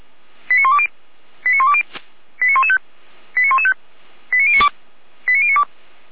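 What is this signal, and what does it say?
Five-tone selective call (ZVEI), heard as quick runs of five tones, tones of about 70 ms. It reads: 89120 89120 89115 89115 89092 89092
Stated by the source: ZVEI five-tone selective call received on an SDR.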